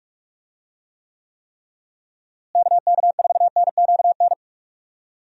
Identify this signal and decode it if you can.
Morse code, heard as a keyed tone, about 46 wpm, 700 Hz. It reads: KK4NQN
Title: USA Call Signs - 6 Characters - Repeated 3x - 45wpm